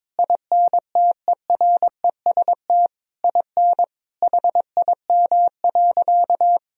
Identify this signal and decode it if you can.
Morse code, heard as a keyed tone, about 22 wpm, 700 Hz. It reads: INTEREST IN HIM.
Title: Sentences from Top 1000 Words - Review Part 2 - 22wpm